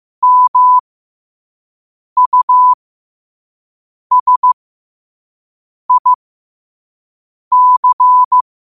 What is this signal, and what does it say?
Morse code, heard as a keyed tone, about 15 wpm, spaced as for 5 wpm, 1000 Hz. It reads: MUSIC
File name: qotdfiles / 5wpm